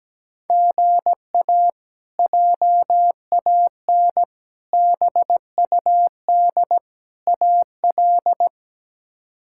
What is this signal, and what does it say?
Morse code, heard as a keyed tone, about 17 wpm, 700 Hz. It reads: GA JAN BUD AL
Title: K1USN SST QSO Practice - 17wpm